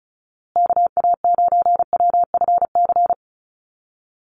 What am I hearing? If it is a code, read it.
Morse code, heard as a keyed tone, about 34 wpm, 700 Hz. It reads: KA9WFC
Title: USA Call Signs Repeated 2x 35wpm